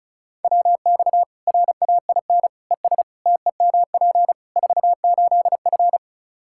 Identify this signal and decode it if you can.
Morse code, heard as a keyed tone, about 36 wpm, 700 Hz. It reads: WX RAIN ES TEMP 48F
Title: Practice QSO - Exchange 2 - Info - 35wpm